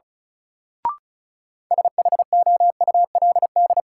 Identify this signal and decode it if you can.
Morse code, heard as a keyed tone, about 34 wpm, 700 Hz. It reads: SHOULD
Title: Sets of 2 Words - New Word Drill - 35wpm